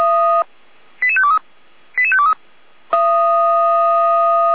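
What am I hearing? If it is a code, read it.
Five-tone selective call (ZVEI), heard as quick runs of five tones, tones of about 70 ms. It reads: 88523 88523
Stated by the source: ZVEI five-tone selective call received on an SDR.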